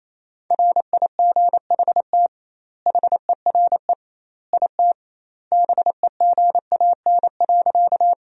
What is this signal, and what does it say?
Morse code, heard as a keyed tone, about 28 wpm, 700 Hz. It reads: RIGHT HERE IT BEGAN.
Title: 2x Word-Spacing - Sentences from Top 300 Words - 28wpm